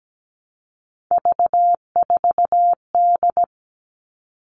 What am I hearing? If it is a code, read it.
Morse code, heard as a keyed tone, about 17 wpm, 700 Hz. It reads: V4D
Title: All Cut Numbers - 3-Digit 17wpm